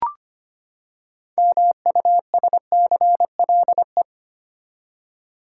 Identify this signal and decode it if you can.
Morse code, heard as a keyed tone, about 25 wpm, 700 Hz. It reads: MUSCLE
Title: Six Letter Words - 25wpm